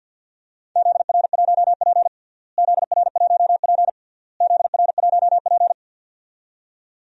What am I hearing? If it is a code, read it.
Morse code, heard as a keyed tone, about 52 wpm, 700 Hz. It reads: ZR1P ZR1P ZR1P
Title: DXCC-100 Call Signs - 4 Characters - Repeated 3x - 50wpm